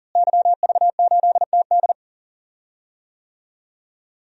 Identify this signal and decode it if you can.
Morse code, heard as a keyed tone, about 38 wpm, 700 Hz. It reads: YV8TD
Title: DXCC-100 Call Signs - 5 Characters - Repeated 2x - 40wpm